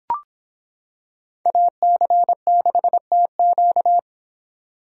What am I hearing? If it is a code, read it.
Morse code, heard as a keyed tone, about 25 wpm, 700 Hz. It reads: AC6TQ